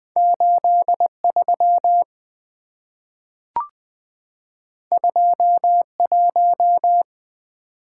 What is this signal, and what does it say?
Morse code, heard as a keyed tone, about 20 wpm, 700 Hz. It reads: 8 3 2 1
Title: Two Letter-Number Combinations - 20wpm